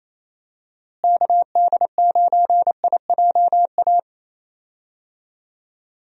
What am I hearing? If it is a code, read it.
Morse code, heard as a keyed tone, about 28 wpm, 700 Hz. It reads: KD9IJA
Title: USA Call Signs - 6 Characters - Repeated 3x - 28wpm